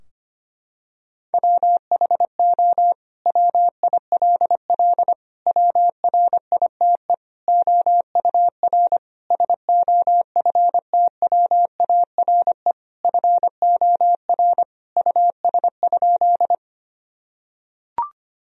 Morse code, heard as a keyed tone, about 25 wpm, 700 Hz. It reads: WHO WILL WRITE OUR SOFTWARE FOR US?
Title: Sentences from Top 1000 Words - Review Part 3 - 25wpm